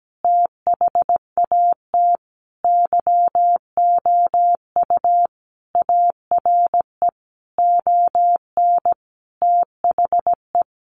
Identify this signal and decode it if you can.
Morse code, heard as a keyed tone, about 17 wpm, 700 Hz. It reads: THAT YOU ARE ON THE
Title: Sentences from Top 1400 Words 17wpm